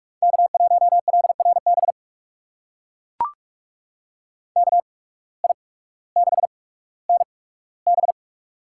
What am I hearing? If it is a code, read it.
Morse code, heard as a keyed tone, about 43 wpm, spaced as for 19 wpm, 700 Hz. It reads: K1LRB KI6NB
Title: USA Call Signs - 5 Characters - 8x Character Spacing - 45wpm